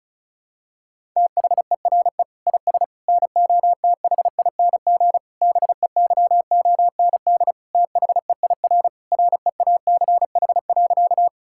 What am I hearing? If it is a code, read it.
Morse code, heard as a keyed tone, about 36 wpm, 700 Hz. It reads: THERE IS NOTHING BEYOND THEIR REACH.
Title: Sentences from Top 1000 Words - Review Part 6 - 35wpm